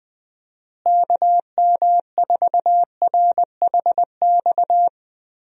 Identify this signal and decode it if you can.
Morse code, heard as a keyed tone, about 20 wpm, 700 Hz. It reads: KM4RHX
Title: USA Call Signs - 6 Characters - Repeated 3x - 20wpm